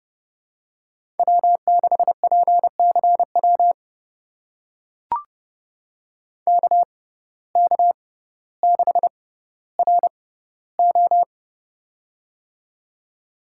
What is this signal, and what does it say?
Morse code, heard as a keyed tone, about 30 wpm, spaced as for 15 wpm, 700 Hz. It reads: W6PCW KK6RO
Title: USA Call Signs - 5 Characters - 6x Character Spacing - 30wpm